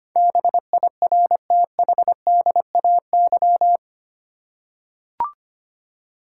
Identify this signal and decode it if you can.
Morse code, heard as a keyed tone, about 25 wpm, 700 Hz. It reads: BIRTHDAY
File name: 8 Letter Words - First 3 Letters Given - 25wpm